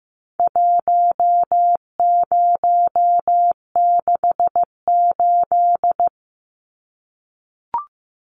Morse code, heard as a keyed tone, about 14 wpm, 700 Hz. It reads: 1068
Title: Numbers - 4-Digit 15wpm